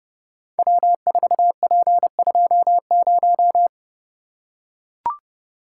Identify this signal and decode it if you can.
Morse code, heard as a keyed tone, about 29 wpm, 700 Hz. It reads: W4P20
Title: Code Groups - 5 Letter:Numbers - 30wpm